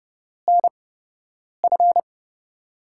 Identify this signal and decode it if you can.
Morse code, heard as a keyed tone, about 30 wpm, spaced as for 8 wpm, 700 Hz. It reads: NF
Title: USA Call Signs - 5 Characters - 8x Character Spacing - 30wpm